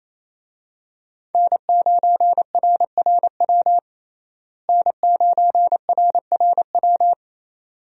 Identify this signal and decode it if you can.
Morse code, heard as a keyed tone, about 28 wpm, 700 Hz. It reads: N9RRW N9RRW
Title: USA Call Signs - 5 Characters - Repeated 2x - 28wpm